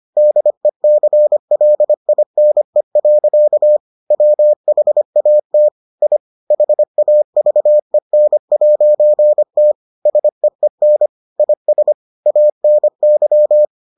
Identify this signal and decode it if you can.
Morse code, heard as a keyed tone, about 25 wpm, 600 Hz. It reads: DECLINE. WHAT I HAVEN'T SEEN IS ANY